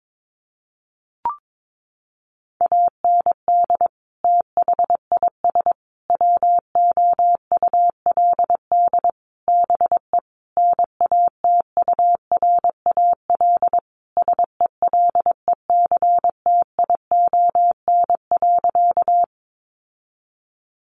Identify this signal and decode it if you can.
Morse code, heard as a keyed tone, about 22 wpm, 700 Hz. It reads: AND THIS WOULD BE NATURAL SELECTION.
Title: Sentences from Top 1400 Words 22wpm